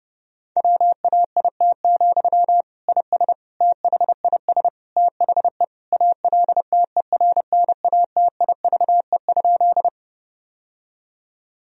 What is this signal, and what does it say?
Morse code, heard as a keyed tone, about 30 wpm, 700 Hz. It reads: WAIT, IS THIS THE ALTERNATIVE?
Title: Sentences from Top 1000 Words - Review Part 4 - 30wpm